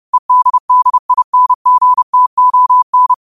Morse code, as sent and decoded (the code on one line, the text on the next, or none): . -.. -.. .. -. --. - --- -.
EDDINGTON